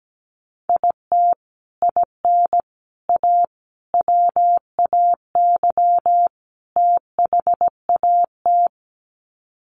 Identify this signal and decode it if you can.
Morse code, heard as a keyed tone, about 17 wpm, 700 Hz. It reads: IT IN A WAY THAT
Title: Sets of 5 Words Encore2 17wpm